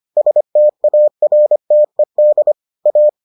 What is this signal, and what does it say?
Morse code, heard as a keyed tone, about 25 wpm, 600 Hz. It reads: STARTED A